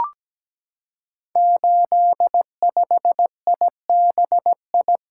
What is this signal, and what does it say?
Morse code, heard as a keyed tone, about 17 wpm, 700 Hz. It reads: 85IBI
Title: Code Groups - 5 Letter:Numbers - 17wpm